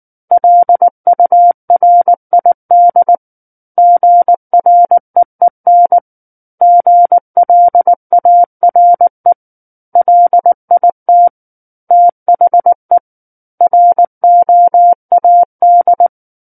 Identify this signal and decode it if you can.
Morse code, heard as a keyed tone, about 19 wpm, 700 Hz. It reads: LURID GREEN GLARE LIT THE ROAD
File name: Book One / 10-Chapter 10 - 19 WPM